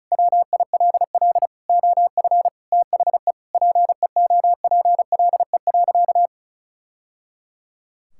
Morse code, heard as a keyed tone, about 36 wpm, 700 Hz. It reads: WILL OF THE PEOPLE.
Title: Sentences from Top 1000 Words - Review Part 5 - 35wpm